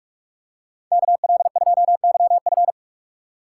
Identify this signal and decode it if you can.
Morse code, heard as a keyed tone, about 48 wpm, 700 Hz. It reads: KL2YF